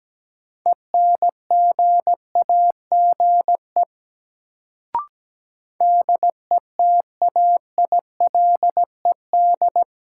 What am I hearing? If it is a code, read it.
Morse code, heard as a keyed tone, about 17 wpm, 700 Hz. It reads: ENGAGE DETAILED
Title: Sentences from Top 1500 Words - New Word Drill - 17wpm